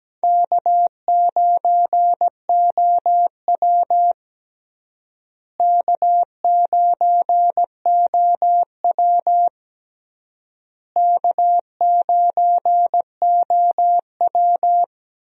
Morse code, heard as a keyed tone, about 17 wpm, 700 Hz. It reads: K9OW K9OW K9OW
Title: USA Call Signs - 4 Characters - Repeated 3x - 17wpm